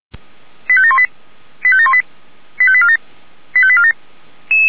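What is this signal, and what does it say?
Five-tone selective call (ZVEI), heard as quick runs of five tones, tones of about 80 ms. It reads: 85718 85718 85847 85847
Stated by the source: ZVEI five-tone call received on SDR, feuerwehr-orlamuende fire brigade alerting.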